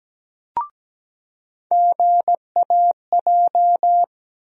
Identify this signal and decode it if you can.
Morse code, heard as a keyed tone, about 17 wpm, 700 Hz. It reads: GAJ
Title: USA Call Sign Postfixes - 3 Letters 17wpm